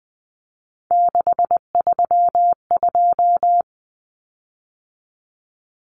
Morse code, heard as a keyed tone, about 20 wpm, 700 Hz. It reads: 632
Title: Common Cut Numbers - 3-Digit 20wpm